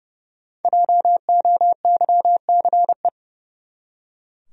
Morse code, heard as a keyed tone, about 31 wpm, 700 Hz. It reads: JOYCE
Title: USA Common Names - 30wpm